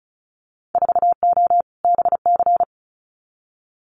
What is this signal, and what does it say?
Morse code, heard as a keyed tone, about 34 wpm, 700 Hz. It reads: 4O BC
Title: Winter Field Day Exchanges Encore 35wpm